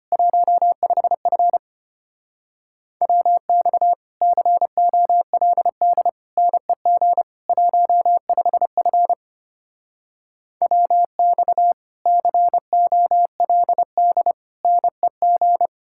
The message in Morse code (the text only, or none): 15F WX COLD NEG 15F WX COLD NEG